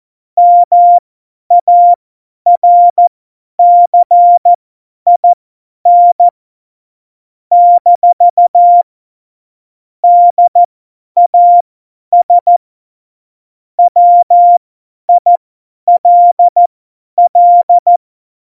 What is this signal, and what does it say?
Morse code, heard as a keyed tone, about 14 wpm, spaced as for 10 wpm, 700 Hz. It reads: MARCIN - DAS WILL
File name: Morsekurs / Witze / Witz 12. Dezember 2017, 50 F70